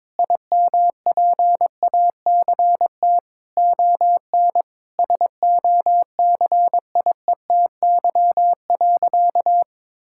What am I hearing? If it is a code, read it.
Morse code, heard as a keyed tone, about 22 wpm, 700 Hz. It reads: IMPACT ON SOCIETY.
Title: Sentences from Top 1000 Words - Review Part 6 - 22wpm